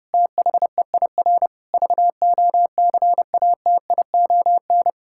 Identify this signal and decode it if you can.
Morse code, heard as a keyed tone, about 30 wpm, 700 Hz. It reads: THEIR VOCATION
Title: Pangram Challenge v2 - 30wpm